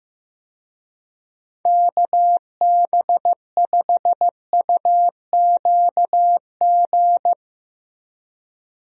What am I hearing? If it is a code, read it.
Morse code, heard as a keyed tone, about 15 wpm, 700 Hz. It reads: KB5UQG
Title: USA Call Signs - 6 Characters - Repeated 2x - 15wpm